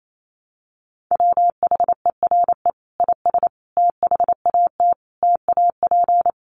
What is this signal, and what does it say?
Morse code, heard as a keyed tone, about 28 wpm, 700 Hz. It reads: WHERE IS THAT TAP
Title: Sentences from Top 1400 Words 28wpm